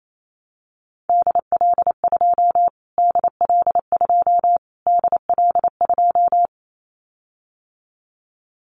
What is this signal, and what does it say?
Morse code, heard as a keyed tone, about 28 wpm, 700 Hz. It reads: DL2 DL2 DL2
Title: Callsign Prefixes for Top 25 DXCC - Repeated 3x - 28wpm